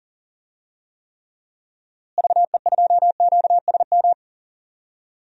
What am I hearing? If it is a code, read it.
Morse code, heard as a keyed tone, about 40 wpm, 700 Hz. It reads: VE2QSM
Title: DXCC-100 Call Signs - 6 Characters - Repeated 2x - 40wpm